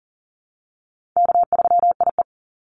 Morse code, heard as a keyed tone, about 40 wpm, 700 Hz. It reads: K3IE
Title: Easy Call Signs Repeated 3x - 40wpm